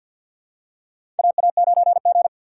Morse code, heard as a keyed tone, about 47 wpm, 700 Hz. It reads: AA9G
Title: USA Call Signs - 6x Character Spacing - 50wpm